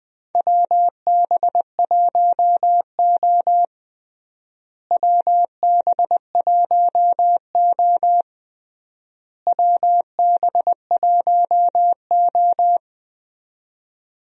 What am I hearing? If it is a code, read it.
Morse code, heard as a keyed tone, about 20 wpm, 700 Hz. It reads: WB1O WB1O WB1O
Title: USA Call Signs - 4 Characters - Repeated 3x - 20wpm